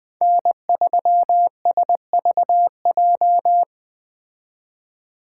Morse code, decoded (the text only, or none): N3SVJ